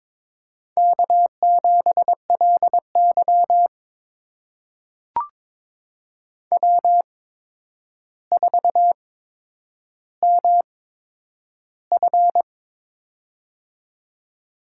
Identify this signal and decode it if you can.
Morse code, heard as a keyed tone, about 22 wpm, spaced as for 9 wpm, 700 Hz. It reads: K7LY W4MF
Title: USA Call Signs - 4 Characters - 8x Character Spacing - 22wpm